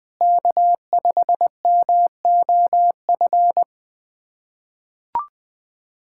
Morse code, heard as a keyed tone, about 20 wpm, 700 Hz. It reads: K5MOF
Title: USA Callsign DE Callsign 20wpm